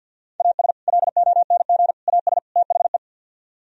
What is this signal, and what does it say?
Morse code, heard as a keyed tone, about 64 wpm, 700 Hz. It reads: AS LONG AS THE